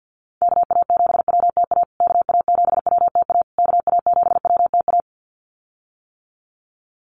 Morse code, heard as a keyed tone, about 50 wpm, 700 Hz. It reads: KA7WTA KA7WTA KA7WTA
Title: USA Call Signs Repeated 3x 50wpm